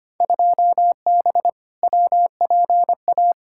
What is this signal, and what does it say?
Morse code, heard as a keyed tone, about 25 wpm, 700 Hz. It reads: 2B WPA
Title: Field Day Exchanges Encore - 25wpm